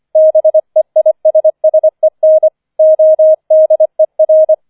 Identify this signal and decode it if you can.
Morse code, heard as a keyed tone, about 25 wpm, 630 Hz. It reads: BEISSEN ODER
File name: 005 Fünfter Streich 25wpm